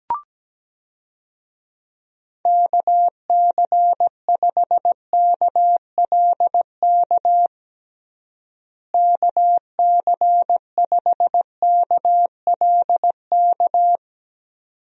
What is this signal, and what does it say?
Morse code, heard as a keyed tone, about 17 wpm, 700 Hz. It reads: KC5KLK KC5KLK
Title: USA Call Signs - 6 Characters - Repeated 2x - 17wpm